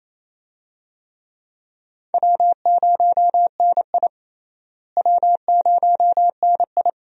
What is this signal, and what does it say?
Morse code, heard as a keyed tone, about 28 wpm, 700 Hz. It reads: W0NI W0NI
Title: USA Call Signs - 4 Characters - Repeated 2x - 28wpm